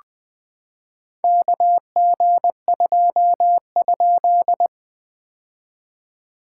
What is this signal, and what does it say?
Morse code, heard as a keyed tone, about 20 wpm, 700 Hz. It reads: KG2?
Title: POTA QSO Practice - 20wpm